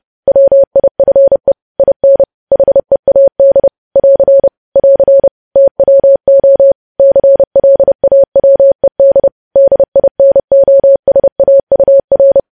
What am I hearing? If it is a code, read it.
Morse code, heard as a keyed tone, about 30 wpm, 550 Hz. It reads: WIFE IN HEAD <AR> <AR> TWO CLAWED DINOSAUR
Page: news headlines in morse code at 30 wpm